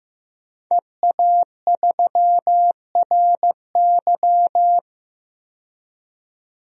Morse code, heard as a keyed tone, about 15 wpm, 700 Hz. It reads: EA3RY
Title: DXCC-100 Call Signs - 5 Characters - 15wpm